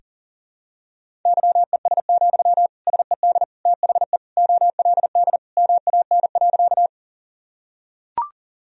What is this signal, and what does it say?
Morse code, heard as a keyed tone, about 40 wpm, 700 Hz. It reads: YES, SED THE OLD MAN.
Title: Sentences from Top 500 Words and Abbreviations 40wpm